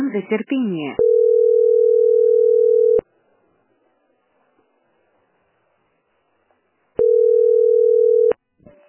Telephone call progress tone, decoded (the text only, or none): ringback tone